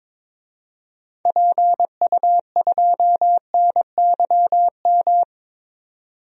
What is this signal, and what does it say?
Morse code, heard as a keyed tone, about 22 wpm, 700 Hz. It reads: PU2NYM
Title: DXCC-100 Call Signs - 6 Characters - 22wpm